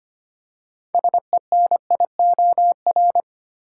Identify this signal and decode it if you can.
Morse code, heard as a keyed tone, about 25 wpm, 700 Hz. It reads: SENIOR
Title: Six Letter Words - 25wpm